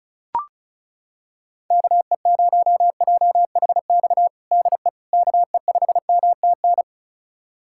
Morse code, heard as a keyed tone, about 36 wpm, 700 Hz. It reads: KE0JHX DE KE5MTN
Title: USA Callsign DE Callsign - 6 Characters - 35wpm